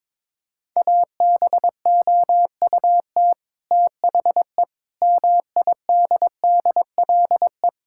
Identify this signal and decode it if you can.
Morse code, heard as a keyed tone, about 22 wpm, 700 Hz. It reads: ABOUT THE MIDDLE
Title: Sentences from Top 1000 Words - Review Part 6 - 22wpm